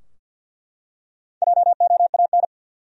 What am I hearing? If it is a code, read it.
Morse code, heard as a keyed tone, about 48 wpm, 700 Hz. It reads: JOAN
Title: USA Common Names - Max 4 Letters - 50wpm